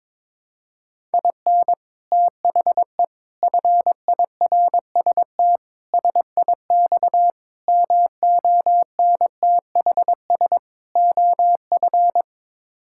Morse code, heard as a keyed tone, about 22 wpm, 700 Hz. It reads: IN THE FIRST SIX MONTHS OF
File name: practice / 22wpm / Sets of 6 Words - 22wpm